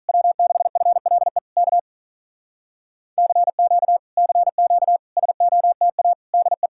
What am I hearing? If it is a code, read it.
Morse code, heard as a keyed tone, about 46 wpm, 700 Hz. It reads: W6FLE K CQ CQ SOTA DE